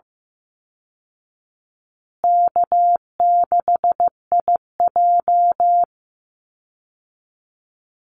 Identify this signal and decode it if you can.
Morse code, heard as a keyed tone, about 15 wpm, 700 Hz. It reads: K6IJ
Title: Easy Call Signs Repeated 3x - 15wpm